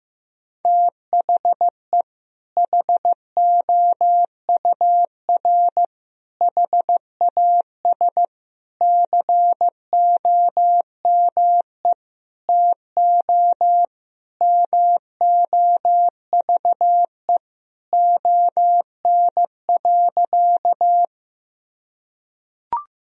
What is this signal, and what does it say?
Morse code, heard as a keyed tone, about 16 wpm, 700 Hz. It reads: THE HOUR HAS COME TO MOVE ON.
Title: Sentences from Top 1000 Words - Review Part 1 - 15wpm